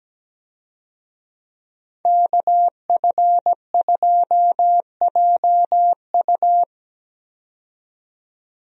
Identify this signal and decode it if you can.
Morse code, heard as a keyed tone, about 17 wpm, 700 Hz. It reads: KF2JU
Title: USA Call Signs - 5 Characters - 17wpm